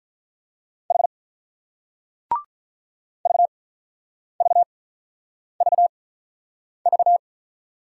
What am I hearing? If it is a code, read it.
Morse code, heard as a keyed tone, about 46 wpm, spaced as for 9 wpm, 690 Hz. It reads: H VVVV